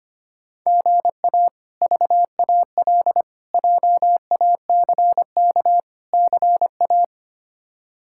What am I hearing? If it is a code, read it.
Morse code, heard as a keyed tone, about 25 wpm, 700 Hz. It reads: GA VAL JACK CA
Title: K1USN SST QSO Practice - 25wpm